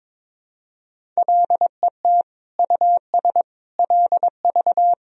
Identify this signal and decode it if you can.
Morse code, heard as a keyed tone, about 22 wpm, 700 Hz. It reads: LET US LV
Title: Sentences from Top 400 Words and Abbreviations 22wpm